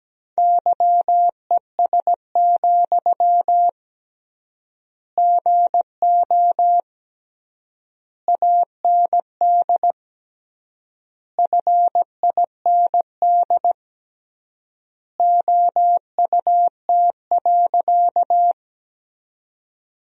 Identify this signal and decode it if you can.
Morse code, heard as a keyed tone, about 17 wpm, 700 Hz. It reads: YES, GO AND FIND OUT.
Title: 3x Word-Spacing - Sentences from Top 500 Words - 17wpm